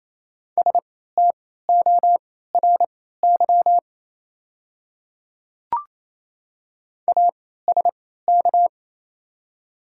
Morse code, heard as a keyed tone, about 28 wpm, spaced as for 16 wpm, 700 Hz. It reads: STORY ASK
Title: 3x Character Spacing - Top 300 Words - 28wpm